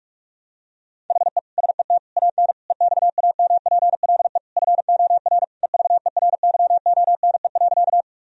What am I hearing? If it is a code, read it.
Morse code, heard as a keyed tone, about 41 wpm, 700 Hz. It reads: HE SET AN EXAMPLE FOR EVERYONE.
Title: Sentences from Top 1000 Words - Review Part 1 - 45wpm